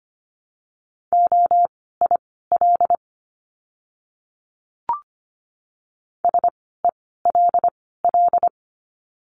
Morse code, heard as a keyed tone, about 25 wpm, spaced as for 16 wpm, 700 Hz. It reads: OIL SELL
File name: course / 25:10wpm / 094 Lesson 16f - Letter c - Words 25wpm